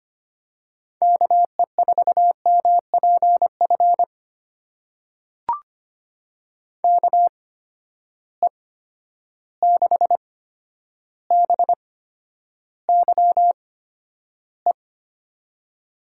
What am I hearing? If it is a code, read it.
Morse code, heard as a keyed tone, about 25 wpm, spaced as for 11 wpm, 700 Hz. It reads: KE4MPF KE6BYE
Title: USA Call Signs - 6 Characters - 8x Character Spacing - 25wpm